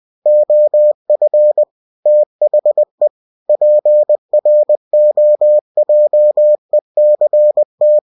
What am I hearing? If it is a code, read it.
Morse code, heard as a keyed tone, about 20 wpm, 600 Hz. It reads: OF THE PROJECT
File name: cw-books / winnie-the-pooh / winnie-20wpm-page-0115